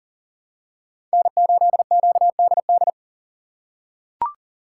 Morse code, heard as a keyed tone, about 40 wpm, 700 Hz. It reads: N8QDD
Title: USA Call Signs - 5 Characters - 4x Character Spacing - 40wpm